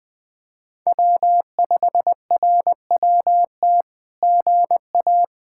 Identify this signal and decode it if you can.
Morse code, heard as a keyed tone, about 20 wpm, 700 Hz. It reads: W5RWT GA
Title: SOTA QSO Practice 20wpm